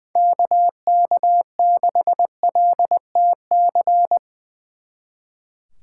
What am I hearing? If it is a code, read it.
Morse code, heard as a keyed tone, about 22 wpm, 700 Hz. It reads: KK6LTC
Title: Speed Racing - USA 2x3 Call Signs 20wpm